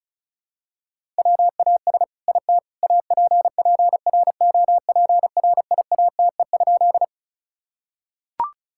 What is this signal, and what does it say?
Morse code, heard as a keyed tone, about 36 wpm, 700 Hz. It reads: WAS IT APPROPRIATE?
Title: Sentences from Top 1000 Words - Review Part 5 - 35wpm